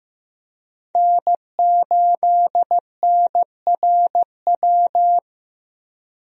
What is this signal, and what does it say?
Morse code, heard as a keyed tone, about 15 wpm, 700 Hz. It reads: N8NRW
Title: USA Call Signs - 5 Characters - Repeated 2x - 15wpm